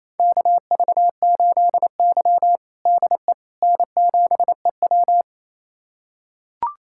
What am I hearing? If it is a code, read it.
Morse code, heard as a keyed tone, about 28 wpm, 700 Hz. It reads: KV8Y DE N7EW
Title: USA Callsign DE Callsign - 4 Characters - 28wpm